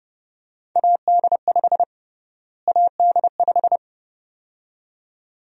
Morse code, heard as a keyed tone, about 30 wpm, 700 Hz. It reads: AD5 AD5
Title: USA Call Sign Prefixes - Repeated 2x - 30wpm